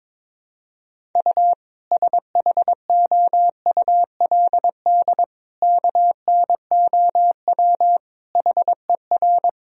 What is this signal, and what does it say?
Morse code, heard as a keyed tone, about 22 wpm, 700 Hz. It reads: U SHOULD KNOW HER
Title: Sentences from Top 400 Words and Abbreviations 22wpm